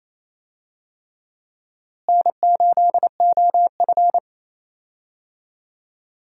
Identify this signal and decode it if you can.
Morse code, heard as a keyed tone, about 28 wpm, 700 Hz. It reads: N8OF
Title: USA Call Signs - 4 Characters - Repeated 3x - 28wpm